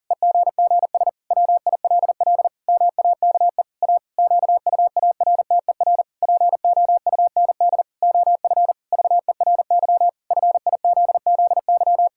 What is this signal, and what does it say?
Morse code, heard as a keyed tone, about 40 wpm, 700 Hz. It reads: EGGS WILL MAKE A QUARTER POUND OF VERY FIZZY
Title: Pangram Challenge v2 - 40wpm